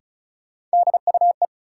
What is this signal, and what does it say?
Morse code, heard as a keyed tone, about 36 wpm, 700 Hz. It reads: DUE
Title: Sets of 5 Words Encore - New Word Drill - 35wpm